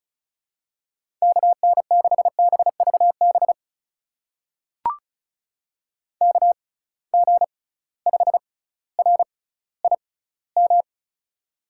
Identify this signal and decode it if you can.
Morse code, heard as a keyed tone, about 36 wpm, spaced as for 18 wpm, 690 Hz. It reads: KN6BVB KG5RIM
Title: USA Call Signs - 6 Characters - 6x Character Spacing - 35wpm